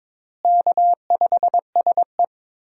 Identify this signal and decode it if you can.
Morse code, heard as a keyed tone, about 22 wpm, 700 Hz. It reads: K5SE
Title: USA Call Signs - 4 Characters - Repeated 3x - 22wpm